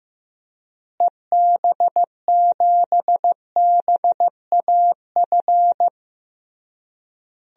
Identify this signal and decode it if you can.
Morse code, heard as a keyed tone, about 15 wpm, 700 Hz. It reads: EB7BAF
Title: DXCC-100 Call Signs - 6 Characters - Repeated 3x - 15wpm